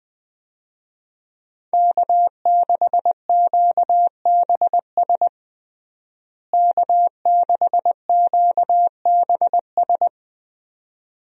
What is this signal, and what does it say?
Morse code, heard as a keyed tone, about 21 wpm, 700 Hz. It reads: K6QBS K6QBS
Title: USA Call Signs - 5 Characters - Repeated 2x - 20wpm